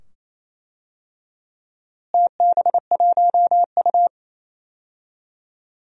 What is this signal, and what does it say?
Morse code, heard as a keyed tone, about 28 wpm, 700 Hz. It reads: TB1U
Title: DXCC-100 Call Signs - 4 Characters - Repeated 2x - 28wpm